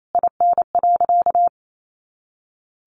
Morse code, heard as a keyed tone, about 28 wpm, 700 Hz. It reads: IN.
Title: Sentences from Top 700 Words 28wpm